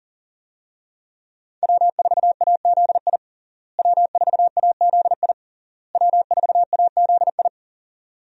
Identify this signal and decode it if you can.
Morse code, heard as a keyed tone, about 37 wpm, 700 Hz. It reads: W4AZI W4AZI W4AZI
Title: USA Call Signs - 5 Characters - Repeated 3x - 40wpm